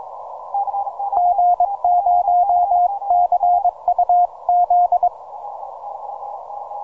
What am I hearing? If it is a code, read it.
Morse code, heard as a keyed tone, about 22 wpm, 710 Hz. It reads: G0CUZ